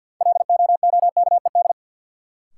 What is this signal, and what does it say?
Morse code, heard as a keyed tone, about 50 wpm, 700 Hz. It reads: LOOKED
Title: Sentences from Top 800 Words - New Word Drill - 50wpm